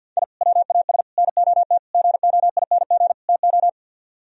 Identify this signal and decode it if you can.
Morse code, heard as a keyed tone, about 59 wpm, 700 Hz. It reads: I WAS NOT GOING TO
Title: Speed-Racing - Sets of 5 Words 50wpm